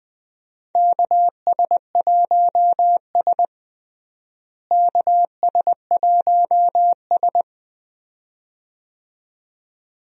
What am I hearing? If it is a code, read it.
Morse code, heard as a keyed tone, about 20 wpm, 700 Hz. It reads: KS1S KS1S
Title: USA Call Signs - 4 Characters - Repeated 2x - 20wpm